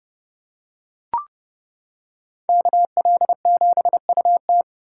Morse code, heard as a keyed tone, about 29 wpm, 700 Hz. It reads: KL7UT